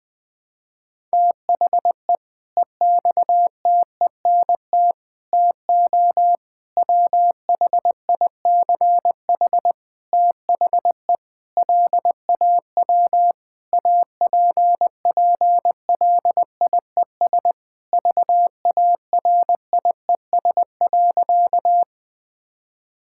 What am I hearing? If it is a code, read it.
Morse code, heard as a keyed tone, about 21 wpm, 700 Hz. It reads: THE EXTENT TO WHICH THE LAW APPLIES VARIES.
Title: Sentences with 5-Word Phrases - Spoken Beginning - 20wpm